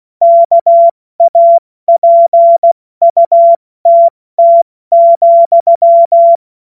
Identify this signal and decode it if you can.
Morse code, heard as a keyed tone, about 16 wpm, 680 Hz. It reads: KAPUTT,
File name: Witz-08-Juli-2025-70-F80